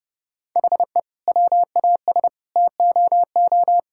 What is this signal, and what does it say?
Morse code, heard as a keyed tone, about 32 wpm, 700 Hz. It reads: HE WAS TOO
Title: Sentences from Top 1000 Words - Review Part 6 - 30wpm